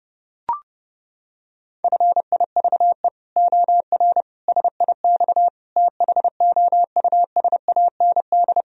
Morse code, heard as a keyed tone, about 30 wpm, 700 Hz. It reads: FIVE OR SIX THOUSAND
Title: Sentences from Top 1000 Words - Review Part 3 - 30wpm